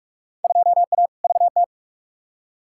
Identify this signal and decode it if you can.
Morse code, heard as a keyed tone, about 44 wpm, 700 Hz. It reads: 2A VT
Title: Field Day Exchanges Encore - 45wpm